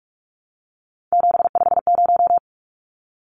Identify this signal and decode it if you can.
Morse code, heard as a keyed tone, about 44 wpm, 700 Hz. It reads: 750
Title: Numbers - 3-Digit 45wpm